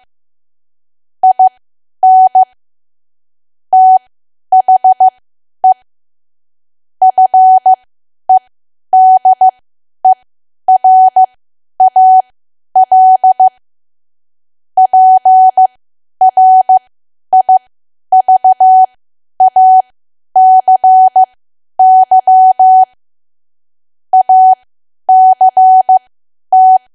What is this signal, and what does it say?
Morse code, heard as a keyed tone, about 15 wpm, spaced as for 10 wpm, 750 Hz. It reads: IN THE FEDERAL PRIVACY ACT
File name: cwa / W1AW-Level2 / 10WPM